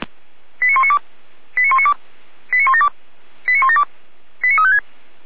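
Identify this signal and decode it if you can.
Five-tone selective call (ZVEI), heard as quick runs of five tones, tones of about 70 ms. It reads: 89192 89192 89172 89172 89356
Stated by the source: ZVEI five-tone call 89350, received by SDR.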